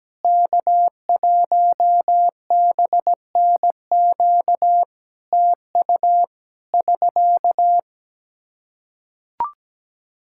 Morse code, heard as a keyed tone, about 17 wpm, 700 Hz. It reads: K1BNQ TU <SK>
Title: K1USN SST QSO Practice 17wpm